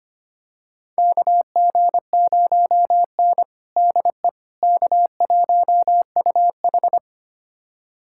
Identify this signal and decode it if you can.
Morse code, heard as a keyed tone, about 25 wpm, 700 Hz. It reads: KG0N DE K1UH
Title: USA Callsign DE Callsign - 4 Characters - 25wpm